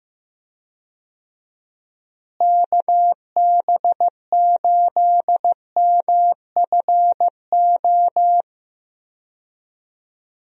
Morse code, heard as a keyed tone, about 15 wpm, 700 Hz. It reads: KB8MFO
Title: USA Call Signs - 6 Characters - Repeated 2x - 15wpm